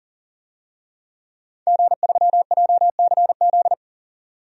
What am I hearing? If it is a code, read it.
Morse code, heard as a keyed tone, about 40 wpm, 700 Hz. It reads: G3JCZ
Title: DXCC-100 Call Signs - 5 Characters - 40wpm